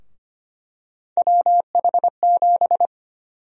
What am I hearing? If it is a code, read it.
Morse code, heard as a keyed tone, about 25 wpm, 700 Hz. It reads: WH7
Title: USA Call Sign Prefixes - Repeated 2x - 25wpm